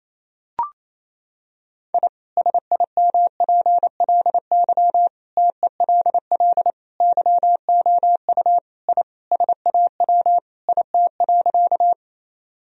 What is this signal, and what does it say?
Morse code, heard as a keyed tone, about 28 wpm, 700 Hz. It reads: I SIMPLY TELL YOU I SAW IT.
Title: Sentences from Top 1000 Words - Review Part 4 - 28wpm